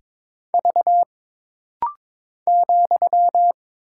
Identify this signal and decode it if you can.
Morse code, heard as a keyed tone, about 22 wpm, 700 Hz. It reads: V ,
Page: This rapid-fire timing is geared to push your ICR (Instant Character Recognition) skills.